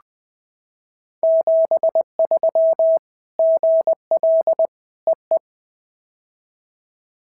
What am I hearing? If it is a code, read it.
Morse code, heard as a keyed tone, about 20 wpm, 650 Hz. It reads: 73 GL EE
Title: POTA QSO Practice - 20wpm